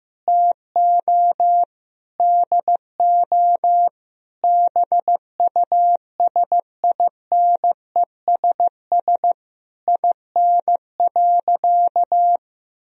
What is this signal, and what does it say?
Morse code, heard as a keyed tone, about 15 wpm, 700 Hz. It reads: TO DO BUSINESS IN.
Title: Sentences from Top 1000 Words - Review Part 3 - 15wpm